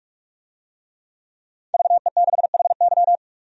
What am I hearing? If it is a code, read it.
Morse code, heard as a keyed tone, about 44 wpm, 700 Hz. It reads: VE6HY
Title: DXCC-100 Call Signs - 5 Characters - 45wpm